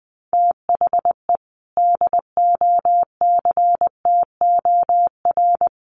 Morse code, heard as a keyed tone, about 20 wpm, 700 Hz. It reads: THE DOCTOR
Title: Sentences from Top 1500 Words 20wpm